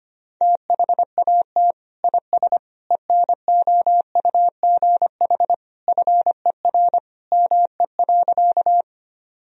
Morse code, heard as a keyed tone, about 25 wpm, 700 Hz. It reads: THAT IS ENOUGH FER ME.
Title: Sentences from Top 400 Words and Abbreviations 25wpm